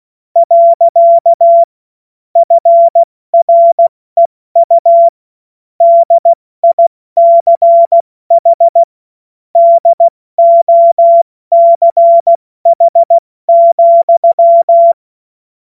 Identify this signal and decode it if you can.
Morse code, heard as a keyed tone, about 16 wpm, 680 Hz. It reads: . FREU DICH DOCH,
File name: Morsekurs / Witze / Witz 06. April 2021, 70 F80